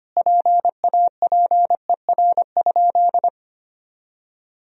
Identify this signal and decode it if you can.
Morse code, heard as a keyed tone, about 25 wpm, 700 Hz. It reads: PAPER?
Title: Sentences from Top 1000 Words - Review Part 2 - 25wpm